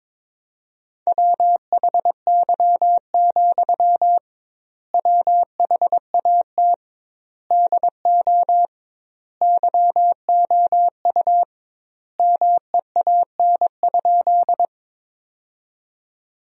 Morse code, heard as a keyed tone, about 22 wpm, 700 Hz. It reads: WHY, WHAT DO YOU MEAN?
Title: 2x Word-Spacing - Sentences from Top 200 Words - 22wpm